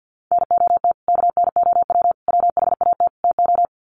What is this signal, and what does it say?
Morse code, heard as a keyed tone, about 50 wpm, 700 Hz. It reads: NOT KNOW WHAT TO
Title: Sets of 4 Words 50wpm